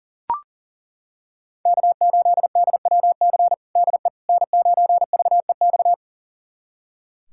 Morse code, heard as a keyed tone, about 40 wpm, 700 Hz. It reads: K8DWC DE N9VEX
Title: USA Callsign DE Callsign - 5 Characters - 40wpm